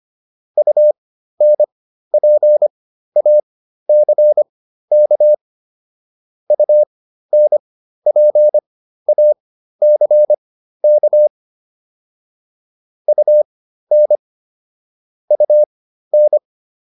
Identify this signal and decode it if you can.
Morse code, heard as a keyed tone, about 25 wpm, spaced as for 13 wpm, 600 Hz. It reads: UNPACK UNPACK UN UN